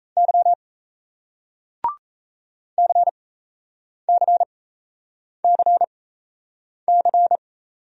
Morse code, heard as a keyed tone, about 38 wpm, spaced as for 8 wpm, 700 Hz. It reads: Y CCCC